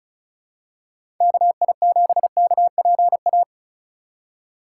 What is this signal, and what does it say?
Morse code, heard as a keyed tone, about 34 wpm, 700 Hz. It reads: KI7KPA
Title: USA Call Signs - 6 Characters - 3x Character Spacing - 35wpm